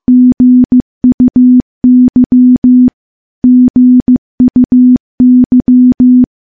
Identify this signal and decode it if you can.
Morse code, heard as a keyed tone, about 15 wpm, 250 Hz. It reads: GUY GUY
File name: morse-training-15WPM